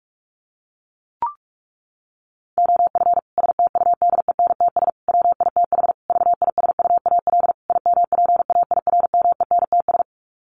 Morse code, heard as a keyed tone, about 44 wpm, 700 Hz. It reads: OF STUDENTS WITH VISUAL IMPAIRMENTS
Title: Sets of 5 Words Encore 45wpm